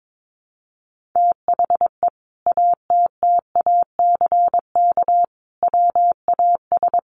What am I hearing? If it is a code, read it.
Morse code, heard as a keyed tone, about 22 wpm, 700 Hz. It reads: THE ATTACK WAS
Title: Sentences from Top 1500 Words 22wpm